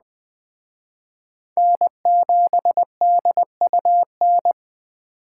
Morse code, heard as a keyed tone, about 24 wpm, 700 Hz. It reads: N7DUN
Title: Speed Racing - USA 1x3 Call Signs 20wpm